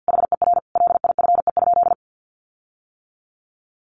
Morse code, heard as a keyed tone, about 50 wpm, 700 Hz. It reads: HER LIFE?
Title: Sentences from Top 1500 Words 50wpm